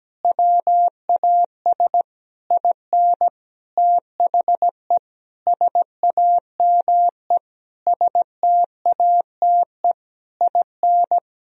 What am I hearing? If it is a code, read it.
Morse code, heard as a keyed tone, about 17 wpm, 700 Hz. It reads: WAS IN THE SAME STATE IN
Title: Sentences from Top 1000 Words - Review Part 2 - 17wpm